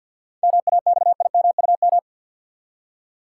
Morse code, heard as a keyed tone, about 46 wpm, 700 Hz. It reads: MAXIMUM
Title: Seven Letter Words - 50wpm